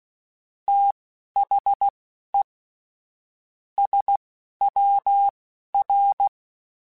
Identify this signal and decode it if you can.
Morse code, heard as a keyed tone, about 16 wpm, spaced as for 12 wpm, 800 Hz. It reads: THE SWR
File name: (spaced) / QSOs Spaced-mp3s / WD0FBT 1 @15 sp